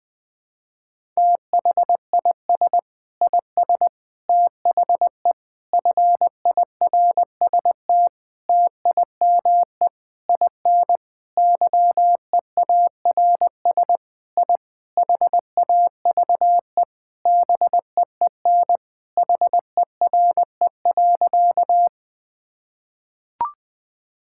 Morse code, heard as a keyed tone, about 20 wpm, 700 Hz. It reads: THIS IS THE FIRST TIME IN YEARS I HAVE BEEN HERE.
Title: Sentences with 4-Word Phrases - 20wpm